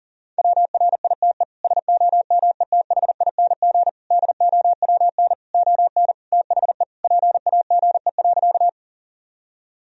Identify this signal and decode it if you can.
Morse code, heard as a keyed tone, about 40 wpm, 700 Hz. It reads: WRITE SOMETHING DOWN ON THE PAGE.
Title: Sentences from Top 1000 Words - Review Part 1 - 40wpm